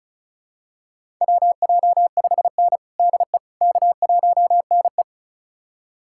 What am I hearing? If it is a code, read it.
Morse code, heard as a keyed tone, about 36 wpm, 700 Hz. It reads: WJ5N DE K1NE
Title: USA Callsign DE Callsign - 4 Characters - 35wpm